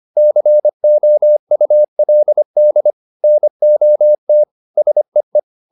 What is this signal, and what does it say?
Morse code, heard as a keyed tone, about 25 wpm, 600 Hz. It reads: COULD NOT SEE